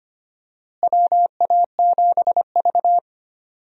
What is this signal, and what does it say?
Morse code, heard as a keyed tone, about 25 wpm, 700 Hz. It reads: WA7V
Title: USA Call Signs - 4 Characters - Repeated 2x - 25wpm